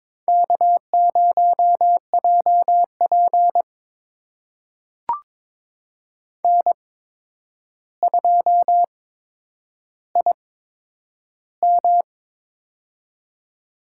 Morse code, heard as a keyed tone, about 22 wpm, spaced as for 9 wpm, 700 Hz. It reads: K0JP N2IM
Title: USA Call Signs - 8x Character Spacing - 22wpm